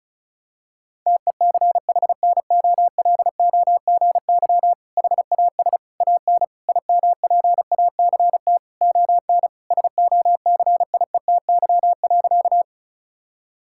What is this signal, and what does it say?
Morse code, heard as a keyed tone, about 36 wpm, 700 Hz. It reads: TECHNOLOGY HAS AN IMPACT ON SOCIETY.
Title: Sentences from Top 1000 Words - Review Part 6 - 35wpm